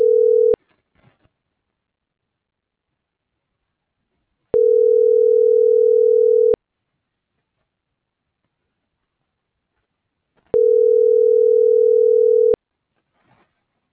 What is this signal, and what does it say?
Ringback tone (North America), 440 + 480 Hz.